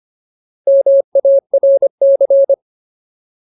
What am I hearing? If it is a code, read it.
Morse code, heard as a keyed tone, about 25 wpm, 550 Hz. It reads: MARC
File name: Ultimate Word List / 25WPM Ultimate